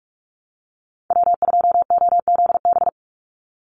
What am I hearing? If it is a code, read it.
Morse code, heard as a keyed tone, about 44 wpm, 700 Hz. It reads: W2OZB
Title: Call Signs Repeated 3x - 45wpm